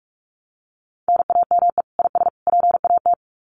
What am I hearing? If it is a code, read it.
Morse code, heard as a keyed tone, about 44 wpm, 700 Hz. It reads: NAME IS PAT
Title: Common QSO Phrases 45wpm